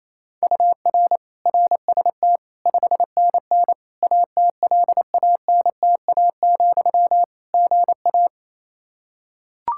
Morse code, heard as a keyed tone, about 28 wpm, 700 Hz. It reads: UR RST 5NN ATLANTA, GA
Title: Practice QSO - Exchange 1 - Beginning - 28wpm